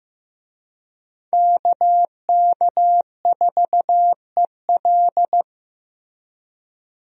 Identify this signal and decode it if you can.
Morse code, heard as a keyed tone, about 15 wpm, 700 Hz. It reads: KK4EL
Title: USA Call Signs - 5 Characters - Repeated 3x - 15wpm